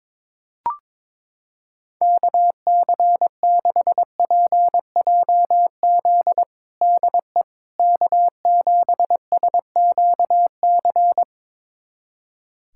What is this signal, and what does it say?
Morse code, heard as a keyed tone, about 22 wpm, 700 Hz. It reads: KC6PJZ DE K7SQC
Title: USA Callsign DE Callsign 22wpm